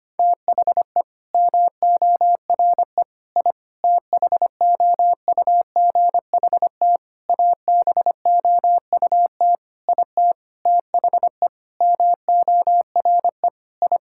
Morse code, heard as a keyed tone, about 25 wpm, 700 Hz. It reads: THE MORE I THOUGHT ABOUT IT THE MORE I
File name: practice / 25wpm / Sentences with 9-Word Phrases - 25wpm